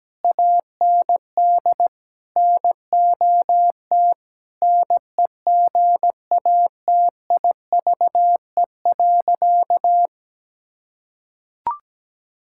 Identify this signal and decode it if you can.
Morse code, heard as a keyed tone, about 17 wpm, 700 Hz. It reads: AND NOT NEGATIVE.
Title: Sentences from Top 1000 Words - Review Part 6 - 17wpm